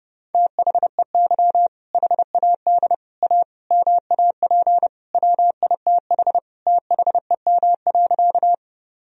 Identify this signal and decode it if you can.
Morse code, heard as a keyed tone, about 30 wpm, 700 Hz. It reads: THEY HAD A MAP WITH THEM.
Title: Sentences from Top 1000 Words - Review Part 2 - 30wpm